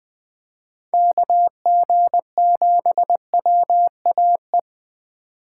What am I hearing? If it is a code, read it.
Morse code, heard as a keyed tone, about 20 wpm, 700 Hz. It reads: KG7WAE
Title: Speed Racing - USA 2x3 Call Signs 17wpm